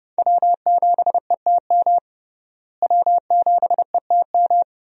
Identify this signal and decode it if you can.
Morse code, heard as a keyed tone, about 30 wpm, 700 Hz. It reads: W7ETM W7ETM
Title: USA Call Signs - 5 Characters - Repeated 2x - 30wpm